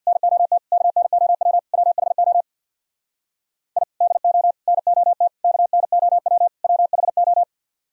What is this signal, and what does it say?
Morse code, heard as a keyed tone, about 62 wpm, 700 Hz. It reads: NOT KNOW WHO I DO NOT KNOW WHO
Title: Speed-Racing - Sets of 5 Words 50wpm